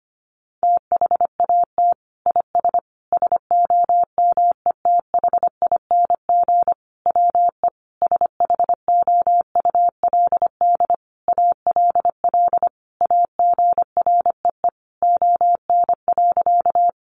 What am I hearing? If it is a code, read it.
Morse code, heard as a keyed tone, about 25 wpm, 700 Hz. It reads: THAT IS SOMETHING WE SHOULD ALL AGREE ON.
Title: Sentences from Top 1100 Words 25wpm